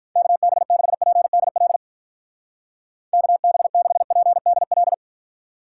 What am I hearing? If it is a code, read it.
Morse code, heard as a keyed tone, about 47 wpm, 700 Hz. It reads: KB6PDL KB6PDL